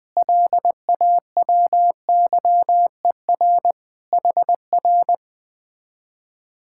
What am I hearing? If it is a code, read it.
Morse code, heard as a keyed tone, about 20 wpm, 700 Hz. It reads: LAWYER HR
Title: Speed-Racing Common QSO Phrases - 17wpm